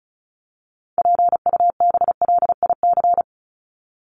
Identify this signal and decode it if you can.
Morse code, heard as a keyed tone, about 34 wpm, 700 Hz. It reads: PUBLIC
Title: Top 501-750 Words 35wpm